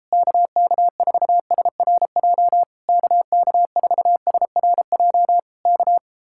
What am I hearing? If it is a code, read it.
Morse code, heard as a keyed tone, about 33 wpm, 700 Hz. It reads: KK4SRJ KK4SRJ K